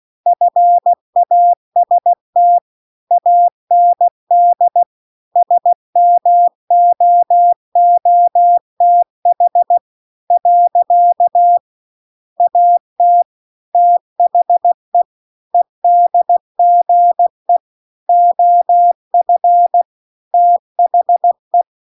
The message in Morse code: FAST AND SMOOTH . AT THE EDGE OF THE

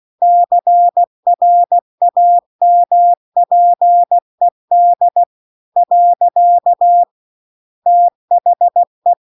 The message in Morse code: CRAMPED . THE

